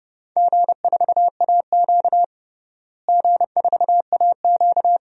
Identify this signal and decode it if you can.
Morse code, heard as a keyed tone, about 29 wpm, 700 Hz. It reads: G4AQ G4AQ
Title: DXCC-100 Call Signs - 4 Characters - Repeated 2x - 30wpm